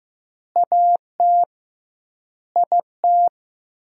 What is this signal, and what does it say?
Morse code, heard as a keyed tone, about 15 wpm, 700 Hz. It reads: AT IT
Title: 2x Word-Spacing - Sentences from Top 200 Words - 15wpm